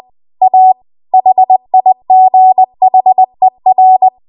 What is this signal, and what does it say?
Morse code, heard as a keyed tone, about 20 wpm, 750 Hz. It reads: A HIGHER
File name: Archive / 20 WPM / 20WPM